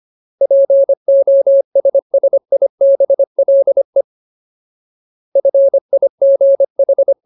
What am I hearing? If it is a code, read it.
Morse code, heard as a keyed tone, about 25 wpm, 550 Hz. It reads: POSSIBLE FIGH
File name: Ultimate Word List / 25WPM Ultimate